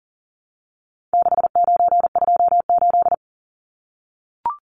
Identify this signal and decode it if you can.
Morse code, heard as a keyed tone, about 40 wpm, 700 Hz. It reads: 6928
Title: Numbers - 4-Digit 40wpm